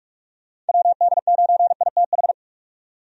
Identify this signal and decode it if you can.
Morse code, heard as a keyed tone, about 46 wpm, 700 Hz. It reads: WD9ITH
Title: USA Call Signs - 6 Characters - 45wpm